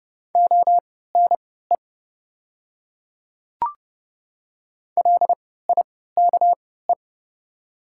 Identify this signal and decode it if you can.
Morse code, heard as a keyed tone, about 30 wpm, spaced as for 17 wpm, 690 Hz. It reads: ONE LIKE
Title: 3x Character Spacing - Top 200 Words - 30wpm